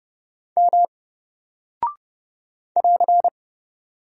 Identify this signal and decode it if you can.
Morse code, heard as a keyed tone, about 29 wpm, 700 Hz. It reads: M <AR>
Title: Single Character (Letter:Number:Punctuation:Prosign) 30wpm